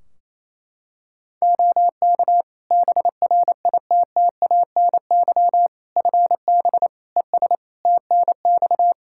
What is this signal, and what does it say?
Morse code, heard as a keyed tone, about 28 wpm, 700 Hz. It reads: OK BRITTANY FB ES TNX
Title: K1USN SST QSO Practice 28wpm